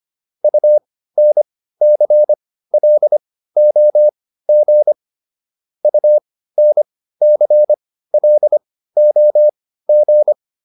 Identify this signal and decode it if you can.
Morse code, heard as a keyed tone, about 25 wpm, spaced as for 15 wpm, 600 Hz. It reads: UNCLOG UNCLOG